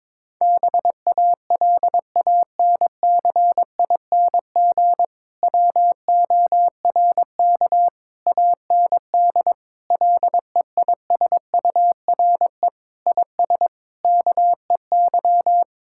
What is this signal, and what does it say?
Morse code, heard as a keyed tone, about 22 wpm, 700 Hz. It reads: BALANCING WORK AND LEISURE IS KEY
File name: Difficult Sentences with Hints - 22wpm